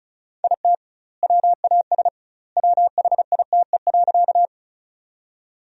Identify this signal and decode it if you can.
Morse code, heard as a keyed tone, about 34 wpm, 700 Hz. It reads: IT WAS WHITE.
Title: 2x Word-Spacing - Sentences from Top 300 Words - 35wpm